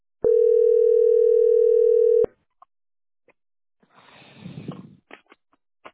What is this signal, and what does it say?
Ringback tone (North America), 440 + 480 Hz.